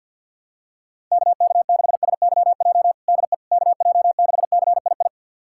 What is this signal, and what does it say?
Morse code, heard as a keyed tone, about 50 wpm, 700 Hz. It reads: KK6SYJ DE KJ6CII
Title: USA Callsign DE Callsign - 6 Characters - 50wpm